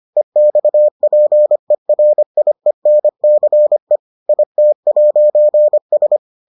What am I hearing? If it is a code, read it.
Morse code, heard as a keyed tone, about 25 wpm, 600 Hz. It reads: EXPERIENCE IT'S